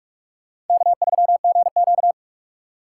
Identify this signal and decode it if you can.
Morse code, heard as a keyed tone, about 46 wpm, 700 Hz. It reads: K3GQ
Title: USA Call Signs - 4 Characters - 3x Character Spacing - 45wpm